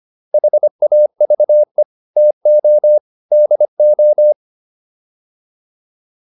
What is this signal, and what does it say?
Morse code, heard as a keyed tone, about 25 wpm, 600 Hz. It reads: HAVE TO DO